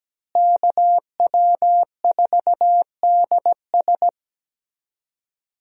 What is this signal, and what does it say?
Morse code, heard as a keyed tone, about 17 wpm, 700 Hz. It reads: KW4DS
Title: USA Call Signs - 5 Characters - Repeated 2x - 17wpm